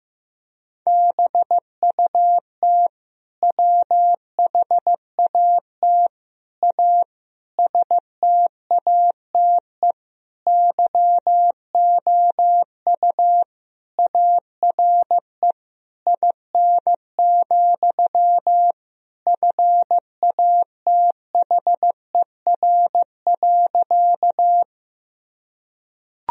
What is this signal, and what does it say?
Morse code, heard as a keyed tone, about 15 wpm, 700 Hz. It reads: BUT WHAT A STATE YOU ARE IN, FATHER.
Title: Sentences from Top 1000 Words - Review Part 2 - 15wpm